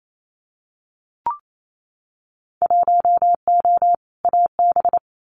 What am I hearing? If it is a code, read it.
Morse code, heard as a keyed tone, about 28 wpm, 690 Hz. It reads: 1O AB
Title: Winter Field Day Exchanges Encore 28wpm